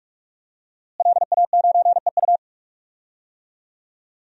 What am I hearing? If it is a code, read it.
Morse code, heard as a keyed tone, about 46 wpm, 700 Hz. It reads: LA9EU
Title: DXCC-100 Call Signs - 5 Characters - Repeated 2x - 45wpm